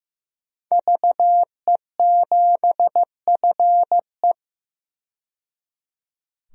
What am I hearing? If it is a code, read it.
Morse code, heard as a keyed tone, about 16 wpm, 700 Hz. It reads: VE7FE
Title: DXCC-100 Call Signs - 5 Characters - 15wpm